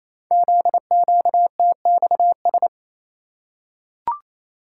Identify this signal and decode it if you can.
Morse code, heard as a keyed tone, about 29 wpm, 700 Hz. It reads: ZQTXS
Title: Code Groups - 5 Letters - 28wpm